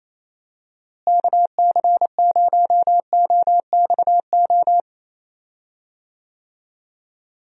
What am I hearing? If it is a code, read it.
Morse code, heard as a keyed tone, about 28 wpm, 700 Hz. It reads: KC0OXO
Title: USA Call Signs - 6 Characters - Repeated 2x - 28wpm